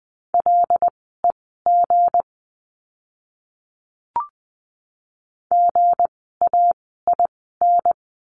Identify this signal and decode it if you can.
Morse code, heard as a keyed tone, about 20 wpm, spaced as for 15 wpm, 700 Hz. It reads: LEG GAIN
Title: 160 Lesson 26f - Letter g - Words 20wpm